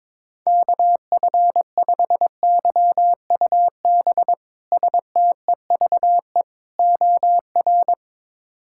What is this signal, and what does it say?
Morse code, heard as a keyed tone, about 22 wpm, 700 Hz. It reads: KF5YUB STEVE OR
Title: K1USN SST QSO Practice - 22wpm